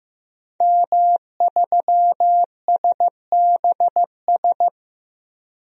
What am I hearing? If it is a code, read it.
Morse code, heard as a keyed tone, about 15 wpm, 700 Hz. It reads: M3SBS
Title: DXCC-100 Call Signs - 5 Characters - 15wpm